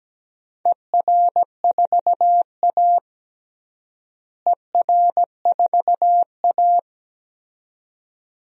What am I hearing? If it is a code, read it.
Morse code, heard as a keyed tone, about 17 wpm, 700 Hz. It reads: ER4A ER4A
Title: DXCC-100 Call Signs - 4 Characters - Repeated 2x - 17wpm